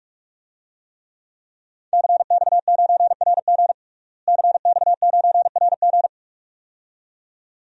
Morse code, heard as a keyed tone, about 46 wpm, 700 Hz. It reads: CX9RG CX9RG
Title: DXCC-100 Call Signs - 5 Characters - Repeated 2x - 45wpm